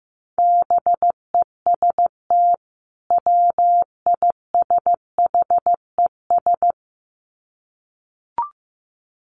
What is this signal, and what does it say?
Morse code, heard as a keyed tone, about 15 wpm, 700 Hz. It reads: BEST WISHES
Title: Common QSO Phrases 15wpm